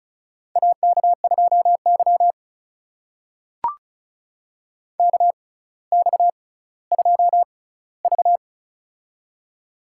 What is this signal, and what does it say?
Morse code, heard as a keyed tone, about 36 wpm, spaced as for 18 wpm, 700 Hz. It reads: AK2Y KX2V
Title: USA Call Signs - 4 Characters - 6x Character Spacing - 35wpm